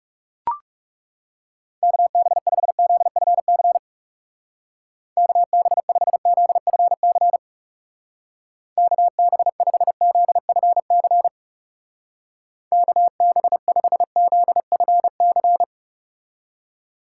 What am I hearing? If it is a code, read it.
Morse code, heard as a keyed tone, about 35 wpm, 700 Hz. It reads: KB5ZFC KB5ZFC KB5ZFC KB5ZFC
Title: Speed Racing - USA 2x3 Call Signs 30wpm